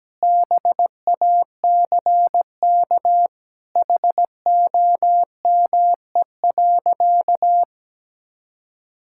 Morse code, heard as a keyed tone, about 17 wpm, 700 Hz. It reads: BACK HOME.
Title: Sentences from Top 1000 Words - Review Part 6 - 17wpm